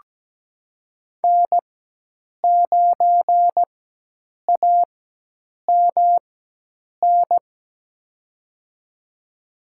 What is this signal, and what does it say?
Morse code, heard as a keyed tone, about 17 wpm, spaced as for 8 wpm, 700 Hz. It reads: N9AMN
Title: USA Call Signs - 5 Characters - 4x Character Spacing - 17wpm